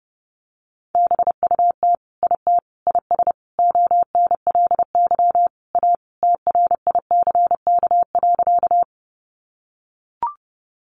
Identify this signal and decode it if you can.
Morse code, heard as a keyed tone, about 30 wpm, 700 Hz. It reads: BUT IT IS ONLY A TRICK.
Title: Sentences from Top 1500 Words 30wpm